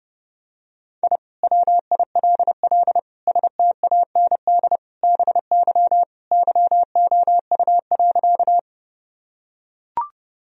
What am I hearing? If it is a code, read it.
Morse code, heard as a keyed tone, about 32 wpm, 700 Hz. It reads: I WILL STAND BY YOU.
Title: Sentences from Top 1000 Words - Review Part 1 - 30wpm